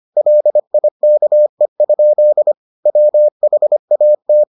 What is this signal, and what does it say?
Morse code, heard as a keyed tone, about 25 wpm, 600 Hz. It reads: LIKE? WHAT